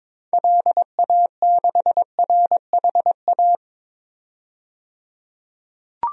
Morse code, heard as a keyed tone, about 22 wpm, 700 Hz. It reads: LA6RHA
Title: DXCC-100 Call Signs - 6 Characters - 22wpm